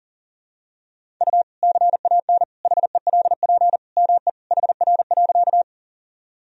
Morse code, heard as a keyed tone, about 40 wpm, 700 Hz. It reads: U CAN HELP ME HR.
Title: Sentences from Top 200 Words and Abbreviations 40wpm